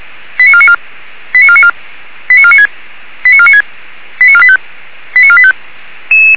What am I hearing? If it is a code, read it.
Five-tone selective call (ZVEI), heard as quick runs of five tones, tones of about 70 ms. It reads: 89494 89494 89496 89496 89485 89485
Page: /464/SDR/ZVEI/89155/2026.04.14/